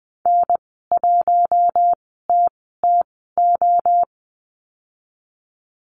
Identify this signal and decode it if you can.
Morse code, heard as a keyed tone, about 20 wpm, spaced as for 14 wpm, 700 Hz. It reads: N1TTO
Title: 042 Lesson 8g - Number 1 - Call Signs 20wpm